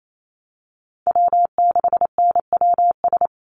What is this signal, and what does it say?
Morse code, heard as a keyed tone, about 28 wpm, 700 Hz. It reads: W6NWS
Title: Call Signs Repeated 2x - 28wpm